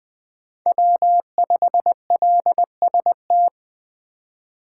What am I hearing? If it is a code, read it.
Morse code, heard as a keyed tone, about 20 wpm, 700 Hz. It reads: W5LST
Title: 2023 POTA Activator Call Signs - 20wpm